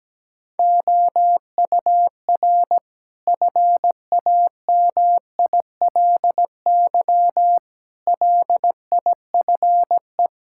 Morse code, heard as a keyed tone, about 17 wpm, 700 Hz. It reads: OUR FAMILY LIFE
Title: Sentences from Top 1000 Words - Review Part 5 - 17wpm